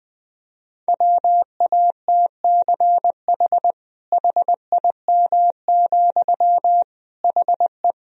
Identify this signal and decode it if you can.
Morse code, heard as a keyed tone, about 21 wpm, 700 Hz. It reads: WATCH HIM, HE
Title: Sentences from Top 300 Words and Abbreviations 20wpm